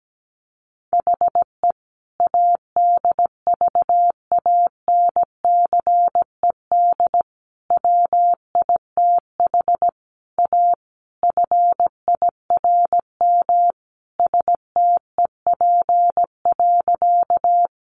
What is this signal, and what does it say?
Morse code, heard as a keyed tone, about 17 wpm, 700 Hz. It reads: HE ADVANCED WITH A FIRM STEP.
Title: Sentences from Top 1400 Words 17wpm